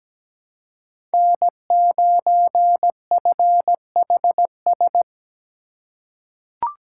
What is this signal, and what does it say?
Morse code, heard as a keyed tone, about 17 wpm, 700 Hz. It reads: N9FHS
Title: USA Call Signs - 5 Characters - 4x Character Spacing - 17wpm